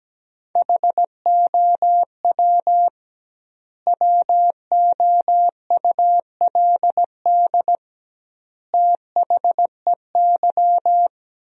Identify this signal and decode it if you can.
Morse code, heard as a keyed tone, about 17 wpm, 700 Hz. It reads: HOW WOULD THEY
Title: 2x Word-Spacing - Sentences from Top 300 Words - 17wpm